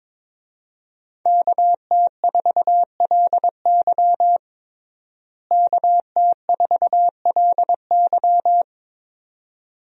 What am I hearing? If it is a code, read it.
Morse code, heard as a keyed tone, about 22 wpm, 700 Hz. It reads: KT4LY KT4LY
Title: USA Call Signs - 5 Characters - Repeated 2x - 22wpm